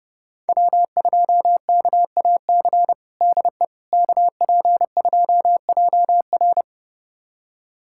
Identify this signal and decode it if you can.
Morse code, heard as a keyed tone, about 32 wpm, 700 Hz. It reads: W2KAC DE KP2JR
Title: USA Callsign DE Callsign - 5 Characters - 30wpm